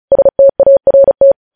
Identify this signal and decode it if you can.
Morse code, heard as a keyed tone, about 36 wpm, 550 Hz. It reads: START
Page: News Headlines in Morse Code at 35 WPM